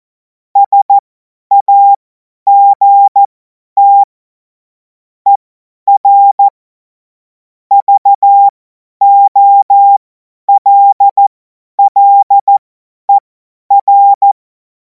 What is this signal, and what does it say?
Morse code, heard as a keyed tone, about 14 wpm, spaced as for 10 wpm, 800 Hz. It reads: SAGT ER VOLLER
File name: Morsekurs / Witze / Witz 28. April 2026 50 F70